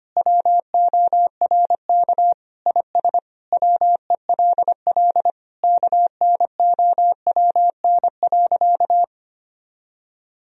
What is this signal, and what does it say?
Morse code, heard as a keyed tone, about 25 wpm, 700 Hz. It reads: WORK IS WELL KNOWN.
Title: Sentences from Top 1000 Words - Review Part 4 - 25wpm